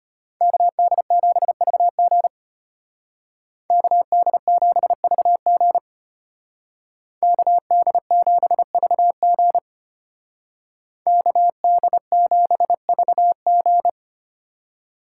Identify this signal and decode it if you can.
Morse code, heard as a keyed tone, about 33 wpm, 700 Hz. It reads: KD7VG KD7VG KD7VG KD7VG